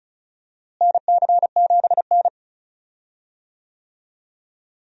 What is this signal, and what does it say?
Morse code, heard as a keyed tone, about 34 wpm, 700 Hz. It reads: NC7N
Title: USA Call Signs - 4 Characters - Repeated 3x - 35wpm